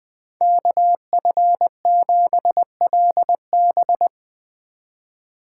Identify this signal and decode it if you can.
Morse code, heard as a keyed tone, about 21 wpm, 700 Hz. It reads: KF7LB
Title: Speed Racing - USA 2x2 Call Signs 17wpm